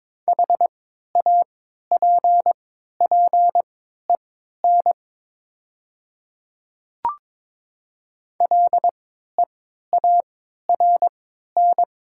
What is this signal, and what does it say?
Morse code, heard as a keyed tone, about 22 wpm, spaced as for 13 wpm, 700 Hz. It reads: HAPPEN LEARN
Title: 3x Character Spacing - Top 500 Words - 22wpm